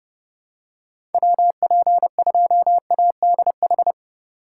Morse code, heard as a keyed tone, about 31 wpm, 700 Hz. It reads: WP2ADH
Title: USA Call Signs - 3x Character Spacing - 30wpm